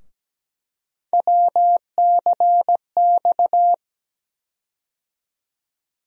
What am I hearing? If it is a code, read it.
Morse code, heard as a keyed tone, about 17 wpm, 700 Hz. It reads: WCX
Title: USA Call Sign Postfixes - 3 Letters 17wpm